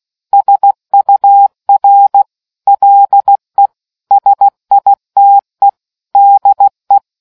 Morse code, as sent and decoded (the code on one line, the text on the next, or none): ... ..- .-. / .-.. / ... .. - . / -.. .
SUR LE SITE DE